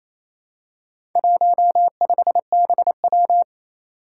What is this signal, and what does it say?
Morse code, heard as a keyed tone, about 28 wpm, 700 Hz. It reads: 15BW
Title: Four Challenging Letter-Number Combinations 28wpm